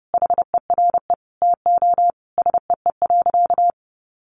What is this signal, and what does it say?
Morse code, heard as a keyed tone, about 30 wpm, 700 Hz. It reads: HERE TO SEE.
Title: Sentences from Top 200 Words 30wpm